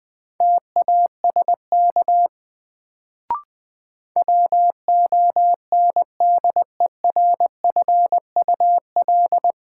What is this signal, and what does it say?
Morse code, heard as a keyed tone, about 20 wpm, 700 Hz. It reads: TASK WONDERFUL
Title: Sentences from Top 1100 Words - New Word Drill Alt - 20wpm